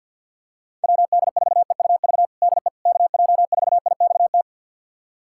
Morse code, heard as a keyed tone, about 53 wpm, 700 Hz. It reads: WD4EUV DE KJ4IXT
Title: USA Callsign DE Callsign - 6 Characters - 50wpm